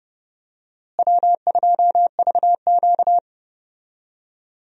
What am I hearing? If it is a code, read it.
Morse code, heard as a keyed tone, about 30 wpm, 700 Hz. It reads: W2VQ
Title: USA Call Signs - 4 Characters - Repeated 3x - 30wpm